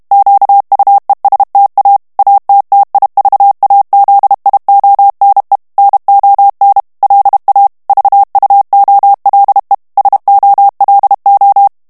Morse code, heard as a keyed tone, about 32 wpm, 800 Hz. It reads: QUESTA ATTIVAZIONE NON LA VUOLE SOLO
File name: ITALIEN texte clair 1 vitesse 30 mots minute